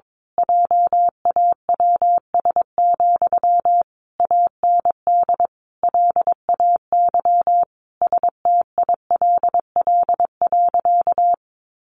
Morse code, heard as a keyed tone, about 22 wpm, 700 Hz. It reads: JAWS, AND LAY STILL.
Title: A Sound Of Thunder 22wpm